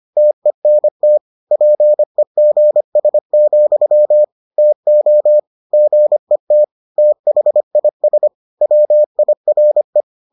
Morse code, heard as a keyed tone, about 25 wpm, 600 Hz. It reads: TENT PEGS, TO GET THIS WIRE